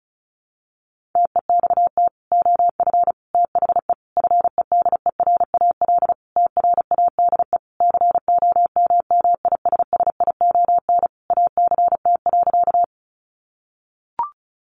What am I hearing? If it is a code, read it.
Morse code, heard as a keyed tone, about 34 wpm, 700 Hz. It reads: TEXT OF THE FEDERAL TRADE COMMISSION ACT.
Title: Sentences from Top 1400 Words 35wpm